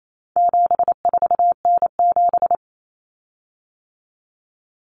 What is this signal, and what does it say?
Morse code, heard as a keyed tone, about 28 wpm, 700 Hz. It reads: 74N7
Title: Common Cut Numbers - 4-Digit 28wpm